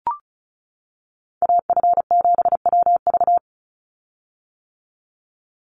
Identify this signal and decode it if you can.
Morse code, heard as a keyed tone, about 36 wpm, 700 Hz. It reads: AF7WV
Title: USA Call Signs 35wpm